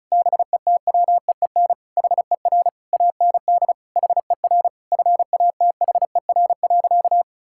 Morse code, heard as a keyed tone, about 36 wpm, 700 Hz. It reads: BETWEEN HER AND HER FATHER.
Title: Sentences from Top 1000 Words - Review Part 6 - 35wpm